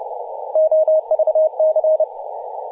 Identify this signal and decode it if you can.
Morse code, heard as a keyed tone, about 30 wpm, 630 Hz, with radio noise. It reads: OVC
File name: posts / 2016-oct-04-2238z-h40gc-18-070mhz-cw